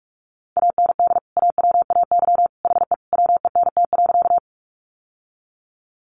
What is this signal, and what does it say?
Morse code, heard as a keyed tone, about 44 wpm, 700 Hz. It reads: AND AWAY HE WENT.
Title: Sentences from Top 600 Words 45wpm